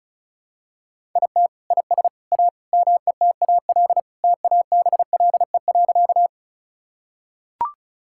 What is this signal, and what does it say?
Morse code, heard as a keyed tone, about 36 wpm, 700 Hz. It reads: IT IS A METAL TABLE.
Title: Sentences from Top 1000 Words - Review Part 6 - 35wpm